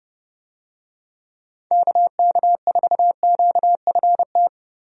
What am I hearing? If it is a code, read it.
Morse code, heard as a keyed tone, about 30 wpm, 700 Hz. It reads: KK4QFT
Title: USA Call Signs - 6 Characters - Repeated 3x - 30wpm